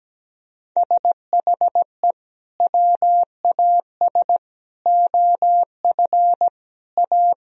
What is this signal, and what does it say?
Morse code, heard as a keyed tone, about 17 wpm, 700 Hz. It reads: SHE WAS OF A
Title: Sentences from Top 1000 Words - Review Part 2 - 17wpm